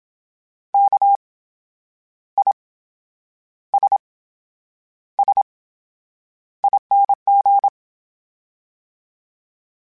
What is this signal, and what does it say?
Morse code, heard as a keyed tone, about 27 wpm, spaced as for 9 wpm, 800 Hz. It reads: KISSING
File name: Special INGs / Special INGs-mp3s / VerySlowToFaster Ing @25